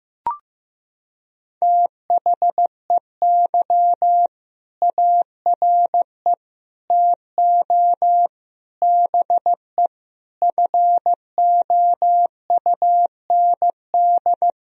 Morse code, heard as a keyed tone, about 15 wpm, 700 Hz. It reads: THEY ARE TO BE FOUND
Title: Sentences from Top 1000 Words - Review Part 3 - 15wpm